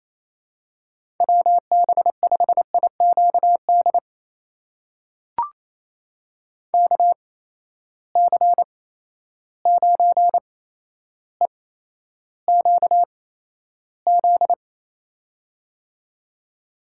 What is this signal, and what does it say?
Morse code, heard as a keyed tone, about 27 wpm, spaced as for 12 wpm, 700 Hz. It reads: WB5IQD KC9EQZ
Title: USA Call Signs - 6 Characters - 8x Character Spacing - 28wpm